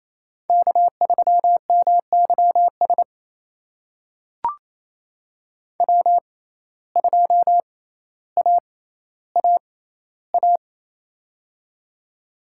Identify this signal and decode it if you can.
Morse code, heard as a keyed tone, about 27 wpm, spaced as for 14 wpm, 700 Hz. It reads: K3MYS W2AAA
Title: USA Call Signs - 5 Characters - 6x Character Spacing - 28wpm